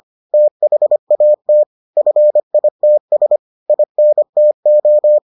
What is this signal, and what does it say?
Morse code, heard as a keyed tone, about 25 wpm, 600 Hz. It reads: THAT FITS INTO